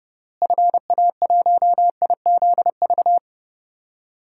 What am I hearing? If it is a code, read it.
Morse code, heard as a keyed tone, about 30 wpm, 700 Hz. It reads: FA1IZV
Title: DXCC-100 Call Signs - 6 Characters - Repeated 3x - 30wpm